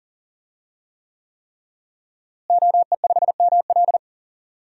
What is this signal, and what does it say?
Morse code, heard as a keyed tone, about 40 wpm, 700 Hz. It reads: OE5ML
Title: DXCC-100 Call Signs - 5 Characters - Repeated 3x - 40wpm